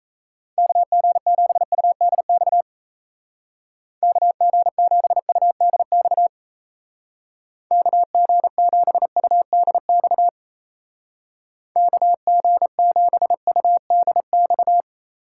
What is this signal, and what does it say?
Morse code, heard as a keyed tone, about 35 wpm, 700 Hz. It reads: KG7UDX KG7UDX KG7UDX KG7UDX
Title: Speed Racing - USA 2x3 Call Signs 28wpm